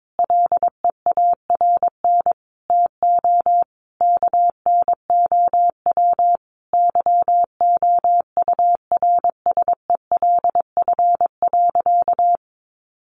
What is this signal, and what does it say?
Morse code, heard as a keyed tone, about 22 wpm, 700 Hz. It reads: LEARN TO KNOW YOURSELF.
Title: Sentences from Top 600 Words 22wpm